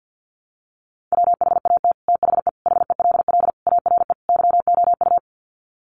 Morse code, heard as a keyed tone, about 50 wpm, 700 Hz. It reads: WHAT THE HELL ARE YOU
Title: Sets of 5 Words Encore 50wpm